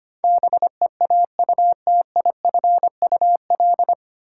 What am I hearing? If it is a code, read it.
Morse code, heard as a keyed tone, about 25 wpm, 700 Hz. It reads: BEAUTIFUL
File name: Nine Letter Words - 25wpm